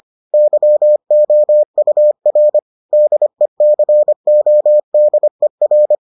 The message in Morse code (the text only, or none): YOUR DECODER